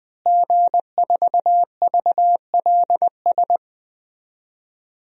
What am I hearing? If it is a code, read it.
Morse code, heard as a keyed tone, about 21 wpm, 700 Hz. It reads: G4VLS
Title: DXCC-100 Call Signs - 5 Characters - Repeated 2x - 20wpm